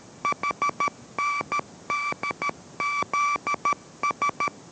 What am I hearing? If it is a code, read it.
Morse code, heard as a keyed tone, about 13 wpm, 1160 Hz. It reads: HNDZS